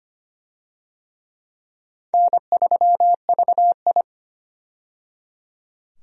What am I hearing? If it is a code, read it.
Morse code, heard as a keyed tone, about 25 wpm, 700 Hz. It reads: N3VI
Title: USA Call Signs - 4 Characters - 25wpm